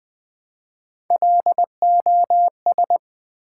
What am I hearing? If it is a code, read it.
Morse code, heard as a keyed tone, about 20 wpm, 700 Hz. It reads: LOS
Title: Build A Word - 1 Letter At A Time - 4-Letter Words - 20wpm